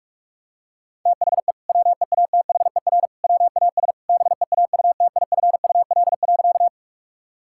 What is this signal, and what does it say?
Morse code, heard as a keyed tone, about 41 wpm, 700 Hz. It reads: THE WEATHER WAS BEAUTIFUL.
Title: Sentences from Top 1000 Words - Review Part 5 - 45wpm